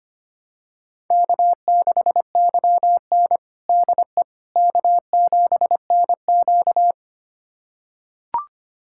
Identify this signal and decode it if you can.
Morse code, heard as a keyed tone, about 25 wpm, 700 Hz. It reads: K6YN DE K7NQ
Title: USA Callsign DE Callsign - 4 Characters - 25wpm